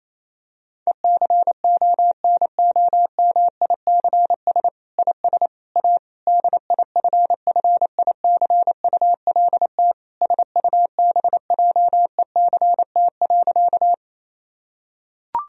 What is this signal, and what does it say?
Morse code, heard as a keyed tone, about 28 wpm, 700 Hz. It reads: ECONOMICS IS A DIFFICULT SUBJECT.
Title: Sentences from Top 1000 Words - Review Part 3 - 28wpm